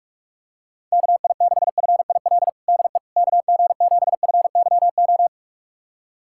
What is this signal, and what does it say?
Morse code, heard as a keyed tone, about 46 wpm, 700 Hz. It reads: KI6FIL DE KG7FYO
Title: USA Callsign DE Callsign 45wpm